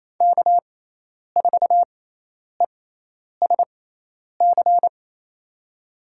Morse code, heard as a keyed tone, about 28 wpm, spaced as for 10 wpm, 700 Hz. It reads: K4ESC
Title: USA Call Signs - 5 Characters - 6x Character Spacing - 28wpm